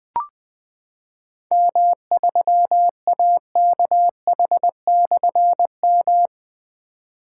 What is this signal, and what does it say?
Morse code, heard as a keyed tone, about 20 wpm, 700 Hz. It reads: M3AKH/M
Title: Difficult Call Signs 20wpm